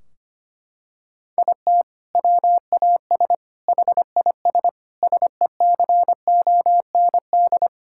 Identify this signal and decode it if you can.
Morse code, heard as a keyed tone, about 25 wpm, 700 Hz. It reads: IT WAS HIS SECOND
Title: Sentences from Top 1600 Words 25wpm